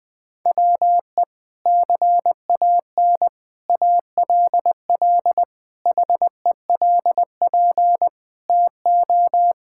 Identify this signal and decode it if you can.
Morse code, heard as a keyed tone, about 20 wpm, 700 Hz. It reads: WE CAN ALL HELP TO
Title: Sentences with 6-Word Phrases - Spoken Beginning - 20wpm